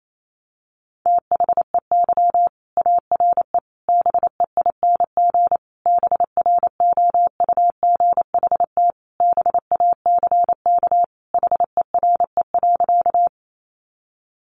Morse code, heard as a keyed tone, about 28 wpm, 700 Hz. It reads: THEY ARE BEING BROUGHT BACK HERE.
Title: Sentences from Top 600 Words 28wpm